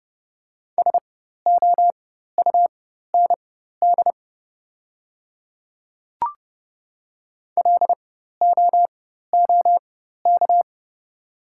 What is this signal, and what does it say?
Morse code, heard as a keyed tone, about 30 wpm, spaced as for 14 wpm, 690 Hz. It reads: SOUND LOOK
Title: 4x Character Spacing - Top 100 Words - 30wpm